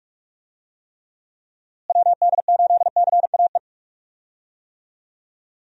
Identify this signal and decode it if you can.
Morse code, heard as a keyed tone, about 46 wpm, 700 Hz. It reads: WD8CAE
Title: USA Call Signs - 6 Characters - Repeated 3x - 45wpm